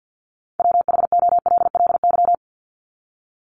Morse code, heard as a keyed tone, about 48 wpm, 700 Hz. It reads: WHOLLY
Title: LY Suffix - Top 100 Words with Base Words 50wpm